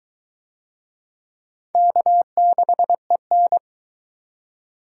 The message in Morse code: K6EN

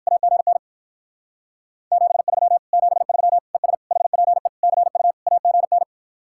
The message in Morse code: AGN 73 73 ES HPE CU AGN

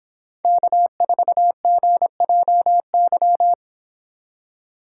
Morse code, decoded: K4GJY

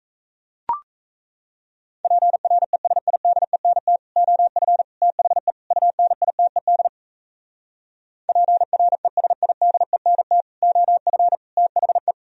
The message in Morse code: PRESIDENT OF THE UNITED PRESIDENT OF THE

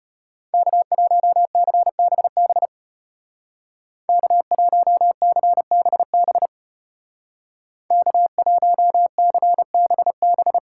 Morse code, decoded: K1CBB K1CBB K1CBB

